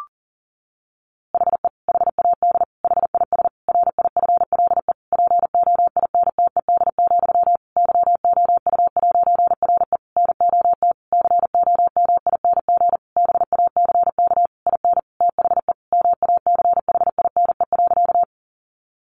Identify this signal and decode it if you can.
Morse code, heard as a keyed tone, about 40 wpm, 700 Hz. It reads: HE HAD HIS RIFLE POINTED, YOU'RE NOT COMING BACK IN THE MACHINE.
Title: A Sound Of Thunder 40wpm